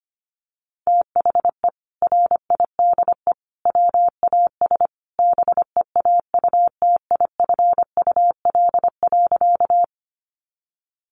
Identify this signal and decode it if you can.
Morse code, heard as a keyed tone, about 25 wpm, 700 Hz. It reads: THE RIDE WAS BEAUTIFUL.
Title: Sentences from Top 1200 Words 25wpm